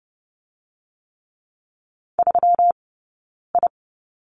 Morse code, heard as a keyed tone, about 30 wpm, 700 Hz. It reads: 3 I
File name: Lesson 20d - Number 3 - Two Characters 30wpm